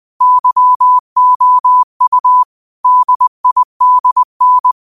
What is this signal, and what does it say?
Morse code, heard as a keyed tone, about 20 wpm, 1000 Hz. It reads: YOU DIDN
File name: 20wpm